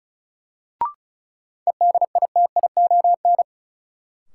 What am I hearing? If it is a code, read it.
Morse code, heard as a keyed tone, about 34 wpm, 700 Hz. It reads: EDITION
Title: Sets of 5 Words - New Word Drill - 35wpm